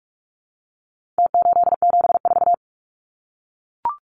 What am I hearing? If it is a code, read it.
Morse code, heard as a keyed tone, about 44 wpm, 700 Hz. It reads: T874
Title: Numbers - Contest Style - 4-Digit 45wpm